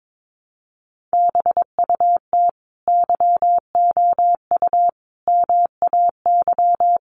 Morse code, heard as a keyed tone, about 22 wpm, 700 Hz. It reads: BUT YOU MAY
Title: Sentences from Top 1100 Words 22wpm